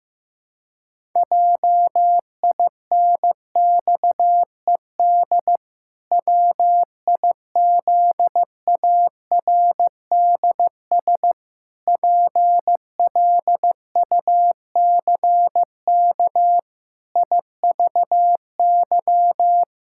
Morse code, heard as a keyed tone, about 16 wpm, 700 Hz. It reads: JINXED WIZARDS PLUCK IVY
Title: Pangram Challenge v2 - 15wpm